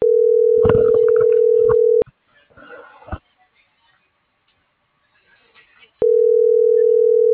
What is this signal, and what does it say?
Ringback tone (North America), 440 + 480 Hz.
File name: CENTRAL SOTOVA / out ricardo.lopez